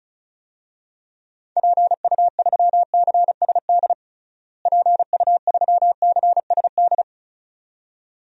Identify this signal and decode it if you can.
Morse code, heard as a keyed tone, about 36 wpm, 700 Hz. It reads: PU3CSD PU3CSD
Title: DXCC-100 Call Signs - 6 Characters - Repeated 2x - 35wpm